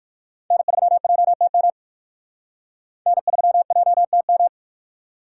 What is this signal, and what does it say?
Morse code, heard as a keyed tone, about 56 wpm, 700 Hz. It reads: N3JTM N3JTM
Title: Speed Racing - USA 1x3 Call Signs 45wpm